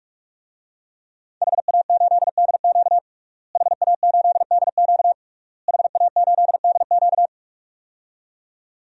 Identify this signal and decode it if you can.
Morse code, heard as a keyed tone, about 46 wpm, 700 Hz. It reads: HA8DQ HA8DQ HA8DQ
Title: DXCC-100 Call Signs - 5 Characters - Repeated 3x - 45wpm